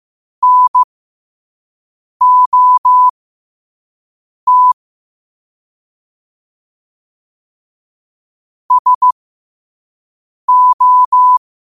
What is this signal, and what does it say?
Morse code, heard as a keyed tone, about 15 wpm, spaced as for 5 wpm, 1000 Hz. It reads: NOT SO
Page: qotd-at-5-wpm